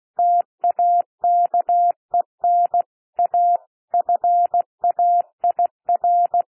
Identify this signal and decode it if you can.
Morse code, heard as a keyed tone, about 16 wpm, 700 Hz. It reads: TAKEN A FAIR
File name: cwmp3 / P1 SOUND PASSAGE 16 wpm